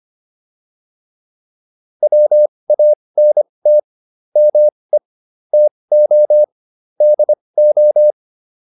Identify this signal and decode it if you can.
Morse code, heard as a keyed tone, about 25 wpm, spaced as for 20 wpm, 600 Hz. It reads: WANT ME TO DO